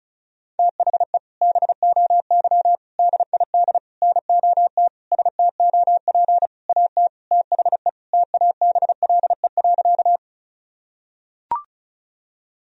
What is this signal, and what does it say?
Morse code, heard as a keyed tone, about 34 wpm, 700 Hz. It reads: THE BOY DID NOT STOP AT THE TABLE.
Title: Sentences from Top 1000 Words - Review Part 2 - 35wpm